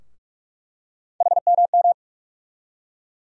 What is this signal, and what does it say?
Morse code, heard as a keyed tone, about 42 wpm, 700 Hz. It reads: HMM
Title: USA Call Sign Postfixes - 3 Letters 45wpm